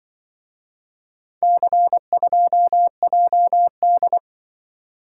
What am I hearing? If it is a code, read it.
Morse code, heard as a keyed tone, about 25 wpm, 700 Hz. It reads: C2JD
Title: Speed-Racing - Four Letter-Number Combinations - 20wpm